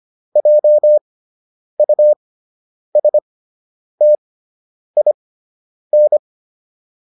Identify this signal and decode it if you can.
Morse code, heard as a keyed tone, about 25 wpm, spaced as for 9 wpm, 600 Hz. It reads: JUSTIN